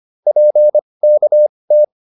Morse code, heard as a keyed tone, about 25 wpm, spaced as for 20 wpm, 600 Hz. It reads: PKT